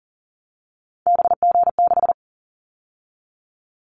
Morse code, heard as a keyed tone, about 40 wpm, 700 Hz. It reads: BG6
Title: Callsign Prefixes for Top 100 DXCC - 40wpm